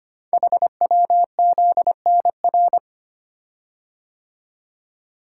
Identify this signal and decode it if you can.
Morse code, heard as a keyed tone, about 25 wpm, 700 Hz. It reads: HWZNR
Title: Code Groups - 5 Letter:Numbers - 25wpm